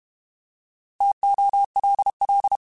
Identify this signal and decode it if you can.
Morse code, heard as a keyed tone, about 32 wpm, 800 Hz. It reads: TOLL
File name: Alphabetical / Alphabetical mp3s / Ts @30